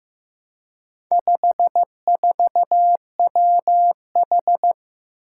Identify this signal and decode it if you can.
Morse code, heard as a keyed tone, about 16 wpm, 700 Hz. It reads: 54WH
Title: Four Challenging Letter-Number Combinations 15wpm